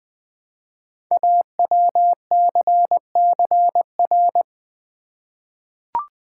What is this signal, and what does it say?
Morse code, heard as a keyed tone, about 20 wpm, 700 Hz. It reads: AWCCR
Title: Code Groups - 5 Letters - 20wpm